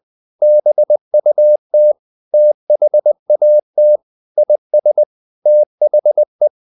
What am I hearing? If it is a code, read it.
Morse code, heard as a keyed tone, about 20 wpm, 600 Hz. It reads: BUT THAT IS THE
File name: the old man and the sea part 2 20wpm